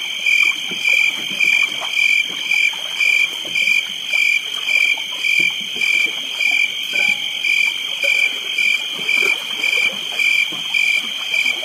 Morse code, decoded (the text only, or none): none